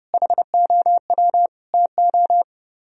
Morse code, heard as a keyed tone, about 28 wpm, 700 Hz. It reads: HOW TO
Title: Speed-Racing - Sets of 5 Words 30wpm